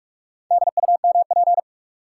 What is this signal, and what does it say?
Morse code, heard as a keyed tone, about 46 wpm, 700 Hz. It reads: DUMP
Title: Sentences from Top 1300 Words - New Word Drill Alt - 45wpm